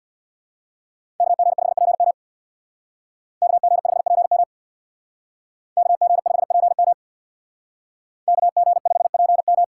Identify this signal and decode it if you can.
Morse code, heard as a keyed tone, about 62 wpm, 700 Hz. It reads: KG5PM KG5PM KG5PM KG5PM
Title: Speed Racing - USA 2x2 Call Signs 50wpm